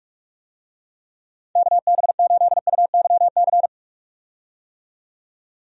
Morse code, heard as a keyed tone, about 46 wpm, 700 Hz. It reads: KB8UYC
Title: USA Call Signs - 6 Characters - 45wpm